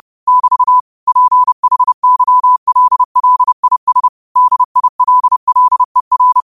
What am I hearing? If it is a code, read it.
Morse code, heard as a keyed tone, about 30 wpm, 1000 Hz. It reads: <BT> PHYLLIS DILLER